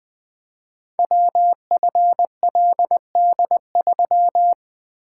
Morse code, heard as a keyed tone, about 20 wpm, 700 Hz. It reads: WFLD3
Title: Code Groups - 5 Letter:Numbers - 20wpm